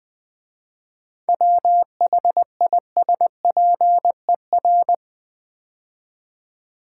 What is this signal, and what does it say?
Morse code, heard as a keyed tone, about 20 wpm, 700 Hz. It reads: WHISPER
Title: 7 Letter Words - First 3 Letters Given - 20wpm